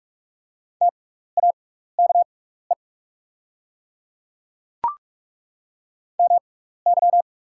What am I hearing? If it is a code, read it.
Morse code, heard as a keyed tone, about 43 wpm, spaced as for 15 wpm, 690 Hz. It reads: TAKE MY
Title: 6x Character Spacing - Top 200 Words - 45wpm